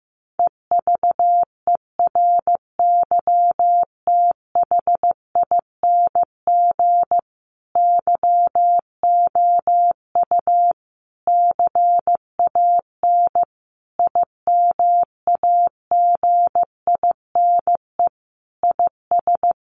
Morse code, heard as a keyed tone, about 14 wpm, 700 Hz. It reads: EVERYTHING YOU CAN IMAGINE IS
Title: Sentences from Top 1400 Words 15wpm